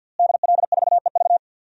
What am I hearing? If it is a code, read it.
Morse code, heard as a keyed tone, about 53 wpm, 700 Hz. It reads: DL4EV
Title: DXCC-100 Call Signs - 5 Characters - Repeated 3x - 50wpm